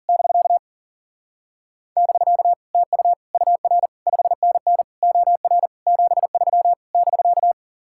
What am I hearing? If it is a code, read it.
Morse code, heard as a keyed tone, about 52 wpm, 700 Hz. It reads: <BK> <BK> TU UR 5NN OR 73 <BK>